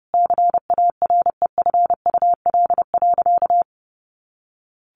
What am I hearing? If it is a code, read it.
Morse code, heard as a keyed tone, about 30 wpm, 700 Hz. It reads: CAREFUL.
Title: A Sound Of Thunder 30wpm